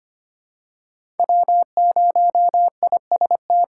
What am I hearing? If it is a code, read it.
Morse code, heard as a keyed tone, about 25 wpm, 700 Hz. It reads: W0IST
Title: USA Call Signs - 5 Characters - 8x Character Spacing - 25wpm